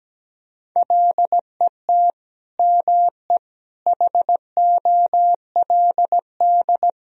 Morse code, heard as a keyed tone, about 17 wpm, 700 Hz. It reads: LET ME HOLD
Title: Sentences from Top 1000 Words - Review Part 2 - 17wpm